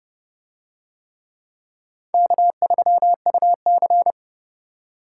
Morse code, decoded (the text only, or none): K3UC